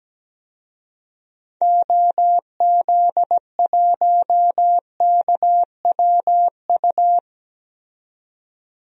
Morse code, heard as a keyed tone, about 17 wpm, 700 Hz. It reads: OZ1KWU
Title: DXCC-100 Call Signs - 6 Characters - 17wpm